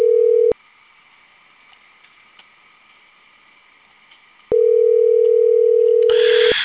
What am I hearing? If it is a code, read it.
Ringback tone (North America), 440 + 480 Hz.